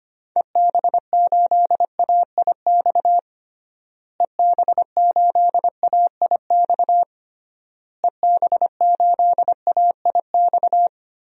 Morse code, heard as a keyed tone, about 25 wpm, 700 Hz. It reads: EB8AIX EB8AIX EB8AIX
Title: DXCC-100 Call Signs - 6 Characters - Repeated 3x - 25wpm